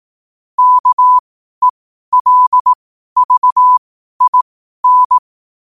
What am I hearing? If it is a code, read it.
Morse code, heard as a keyed tone, about 18 wpm, spaced as for 12 wpm, 1000 Hz. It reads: KELVIN
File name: qotdfiles / 12wpm